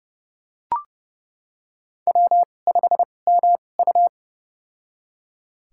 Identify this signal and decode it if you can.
Morse code, heard as a keyed tone, about 31 wpm, spaced as for 22 wpm, 700 Hz. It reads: W5MU
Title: USA Call Signs - 4 Characters - 2x Character Spacing - 30wpm